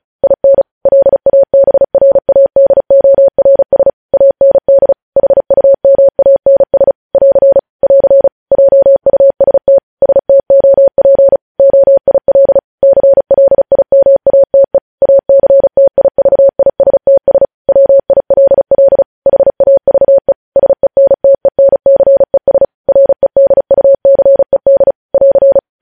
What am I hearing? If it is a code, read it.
Morse code, heard as a keyed tone, about 36 wpm, 550 Hz. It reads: IN LABRADORS AND HUMANS <AR> <AR> JUST STOP OIL CLIMATE ACTIVISTS WILL HAVE SENTENCES REDUCED <AR>
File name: podcast / News headlines-1x-35x35